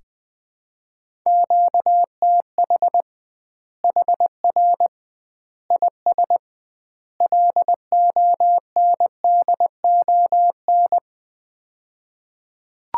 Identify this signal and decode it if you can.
Morse code, heard as a keyed tone, about 21 wpm, 700 Hz. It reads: QTH HR IS LONDON
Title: Common QSO Phrases - 2x Word-Spacing - 20wpm